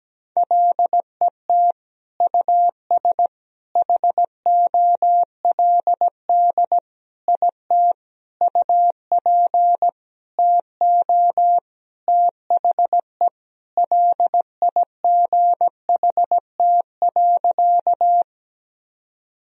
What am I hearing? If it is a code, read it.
Morse code, heard as a keyed tone, about 17 wpm, 700 Hz. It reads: LET US HOLD IT UP TO THE LIGHT.
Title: Sentences from Top 1000 Words - Review Part 2 - 17wpm